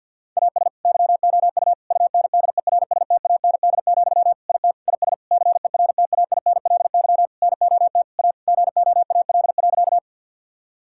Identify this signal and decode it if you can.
Morse code, heard as a keyed tone, about 50 wpm, 700 Hz. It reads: AS YOU UNDERSTAND, IT IS CERTAINLY NOT A GOAL.
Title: Sentences from Top 1000 Words - Review Part 5 - 50wpm